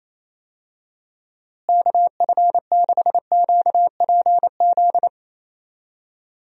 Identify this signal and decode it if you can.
Morse code, heard as a keyed tone, about 28 wpm, 700 Hz. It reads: KF6QPZ
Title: USA Call Signs - 6 Characters - Repeated 3x - 28wpm